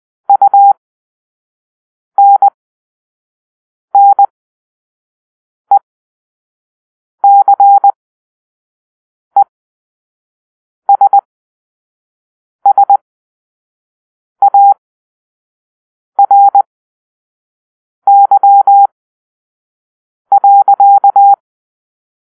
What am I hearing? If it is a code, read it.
Morse code, heard as a keyed tone, about 20 wpm, spaced as for 5 wpm, 800 Hz. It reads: UNNECESSARY.
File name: Grp-3 Drill-1